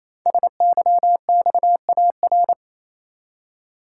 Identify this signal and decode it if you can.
Morse code, heard as a keyed tone, about 29 wpm, 700 Hz. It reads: SYXAR
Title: Code Groups - 5 Letters - 28wpm